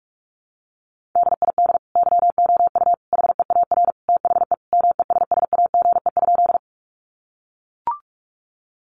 Morse code, heard as a keyed tone, about 44 wpm, 700 Hz. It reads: DID YOU HEAR THE MESSAGE?
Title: Sentences from Top 1100 Words 45wpm